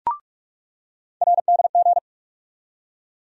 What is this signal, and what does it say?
Morse code, heard as a keyed tone, about 47 wpm, 700 Hz. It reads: RDG
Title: USA Call Sign Postfixes - 3 Letters 45wpm